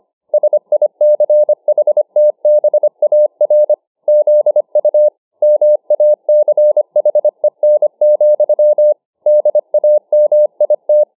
Morse code, heard as a keyed tone, about 25 wpm, 600 Hz. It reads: SICHTBAR ZU MACHEN, DAMIT